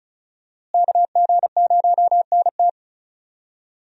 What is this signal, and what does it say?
Morse code, heard as a keyed tone, about 34 wpm, 700 Hz. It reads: KG0NT
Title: USA Call Signs - 5 Characters - Repeated 2x - 35wpm